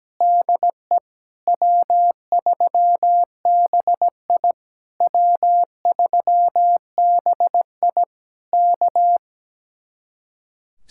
Morse code, heard as a keyed tone, about 17 wpm, 700 Hz. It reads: DE W3BI W3BI K
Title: Speed-Racing Common QSO Phrases - 17wpm